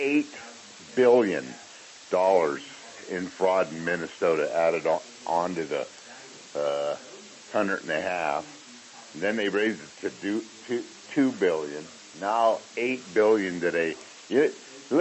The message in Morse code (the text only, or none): none